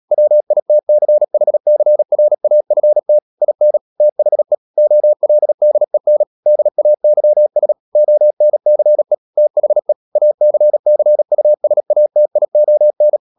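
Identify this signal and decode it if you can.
Morse code, heard as a keyed tone, about 37 wpm, 600 Hz. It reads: WITCHCRAFT IN THE OLDEN DAYS ONCE THE ACCUSATION
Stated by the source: Morse code practice at 35–39 wpm sending English text of Ken Follett's novel Never, from CW med Gnister.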